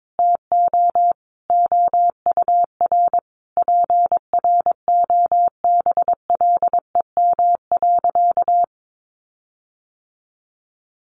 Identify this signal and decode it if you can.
Morse code, heard as a keyed tone, about 22 wpm, 700 Hz. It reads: TO OUR PROBLEM.
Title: Sentences from Top 1300 Words 22wpm